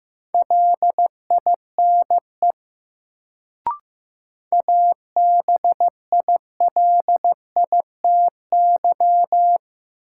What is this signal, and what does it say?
Morse code, heard as a keyed tone, about 15 wpm, 700 Hz. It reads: LINE ABILITY
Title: Sentences from Top 1200 Words - New Word Drill Alt - 15wpm